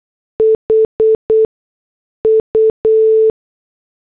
Morse code, heard as a keyed tone, about 8 wpm, spaced as for 6 wpm, 440 Hz. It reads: HU
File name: morse / headlines